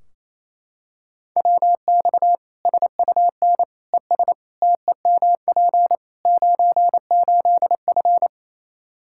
Morse code, heard as a keyed tone, about 28 wpm, 700 Hz. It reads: WX SUN ES TEMP 98F
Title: K1USN SST QSO Practice 28wpm